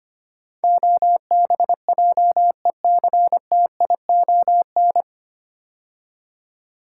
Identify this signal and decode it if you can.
Morse code, heard as a keyed tone, about 25 wpm, 700 Hz. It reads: OBJECTION
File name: practice / 25wpm / 9 Letter Words - First 3 Letters Given - 25wpm